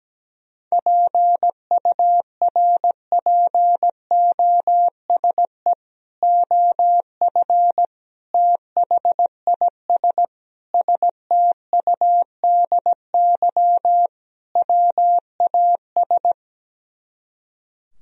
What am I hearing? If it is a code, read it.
Morse code, heard as a keyed tone, about 17 wpm, 700 Hz. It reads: PURPOSE OF THIS STUDY WAS
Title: Speed-Racing - Sets of 5 Words 17wpm